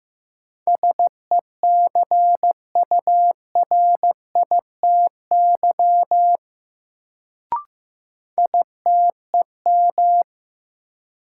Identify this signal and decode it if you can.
Morse code, heard as a keyed tone, about 15 wpm, 700 Hz. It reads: SECURITY ITEM
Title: Sentences from Top 1400 Words - New Word Drill Alt - 15wpm